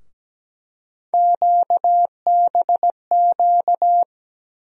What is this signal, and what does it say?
Morse code, heard as a keyed tone, about 17 wpm, 700 Hz. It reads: QBQ
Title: USA Call Sign Postfixes - 3 Letters 17wpm